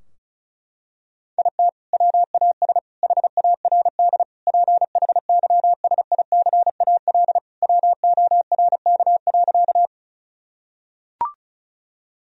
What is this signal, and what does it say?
Morse code, heard as a keyed tone, about 36 wpm, 700 Hz. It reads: IT WAS HARD PHYSICAL WORK.
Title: Sentences from Top 1000 Words - Review Part 4 - 35wpm